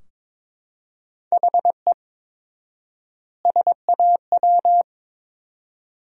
Morse code, heard as a keyed tone, about 21 wpm, 700 Hz. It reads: HE SAW
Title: 4x Word-Spacing - Sentences from Top 300 Words - 22wpm